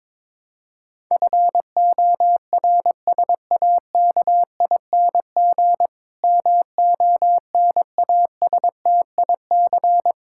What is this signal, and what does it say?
Morse code, heard as a keyed tone, about 22 wpm, 700 Hz. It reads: FORSAKING MONASTIC
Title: Pangram Challenge v2 - 22wpm